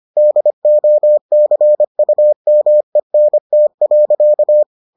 Morse code, heard as a keyed tone, about 25 wpm, 600 Hz. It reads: DOCUMENT.